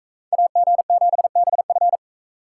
Morse code, heard as a keyed tone, about 44 wpm, 700 Hz. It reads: AG7BF